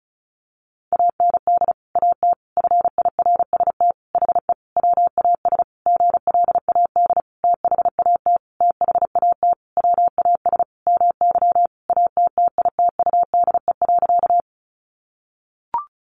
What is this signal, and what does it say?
Morse code, heard as a keyed tone, about 34 wpm, 700 Hz. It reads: AND AT FIRST HE WAS GLAD THAT THAT WAS MY ATTITUDE.
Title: Sentences from Top 1300 Words 35wpm